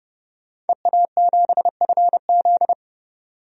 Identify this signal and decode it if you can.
Morse code, heard as a keyed tone, about 31 wpm, 700 Hz. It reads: EA7FZ
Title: DXCC-100 Call Signs - 5 Characters - 30wpm